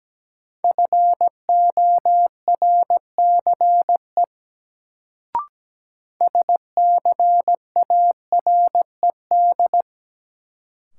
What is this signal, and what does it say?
Morse code, heard as a keyed tone, about 17 wpm, 700 Hz. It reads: FORCE SCARED
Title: Sentences from Top 700 Words - New Word Drill - 17wpm